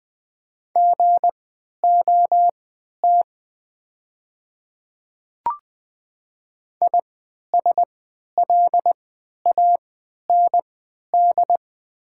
Morse code, heard as a keyed tone, about 20 wpm, spaced as for 11 wpm, 690 Hz. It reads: GOT ISLAND
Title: 3x Character Spacing - Top 500 Words - 20wpm